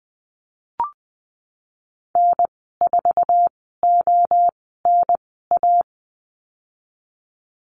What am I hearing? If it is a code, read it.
Morse code, heard as a keyed tone, about 20 wpm, spaced as for 15 wpm, 700 Hz. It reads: N4ONA
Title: 048 Lesson 9g - Number 4 - Call Signs 20wpm